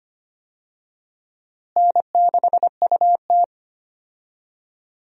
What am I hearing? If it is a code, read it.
Morse code, heard as a keyed tone, about 25 wpm, 700 Hz. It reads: N6UT
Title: USA Call Signs - 4 Characters - Repeated 2x - 25wpm